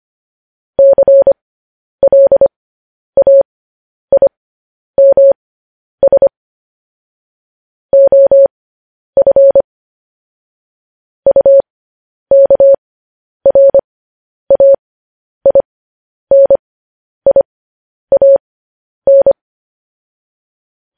Morse code, heard as a keyed tone, about 25 wpm, spaced as for 10 wpm, 550 Hz. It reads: CLAIMS OF UKRAINIAN
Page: News Headlines in Morse Code at 10 WPM